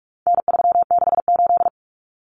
Morse code, heard as a keyed tone, about 44 wpm, 700 Hz. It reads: N368
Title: Common Cut Numbers - 4-Digit 45wpm